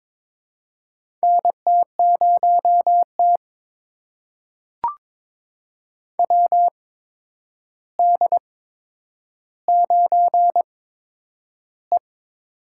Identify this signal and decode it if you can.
Morse code, heard as a keyed tone, about 22 wpm, spaced as for 9 wpm, 700 Hz. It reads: NT0T WD9E
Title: USA Call Signs - 8x Character Spacing - 22wpm